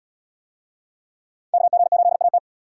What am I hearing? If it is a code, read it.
Morse code, heard as a keyed tone, about 73 wpm, 700 Hz. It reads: KG8AT